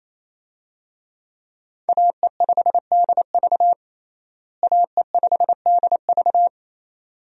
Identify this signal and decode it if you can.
Morse code, heard as a keyed tone, about 27 wpm, 700 Hz. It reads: AE5DV AE5DV
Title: USA Call Signs - 5 Characters - Repeated 2x - 28wpm